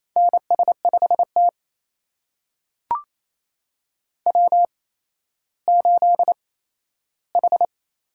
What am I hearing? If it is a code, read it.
Morse code, heard as a keyed tone, about 28 wpm, spaced as for 14 wpm, 700 Hz. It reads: NS5T W8H
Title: USA Call Signs - 4 Characters - 8x Character Spacing - 28wpm